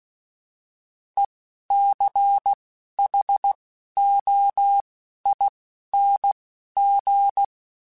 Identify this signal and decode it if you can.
Morse code, heard as a keyed tone, about 16 wpm, spaced as for 11 wpm, 800 Hz. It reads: ECHOING
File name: Alphabetical / Alphabetical mp3s / Es @10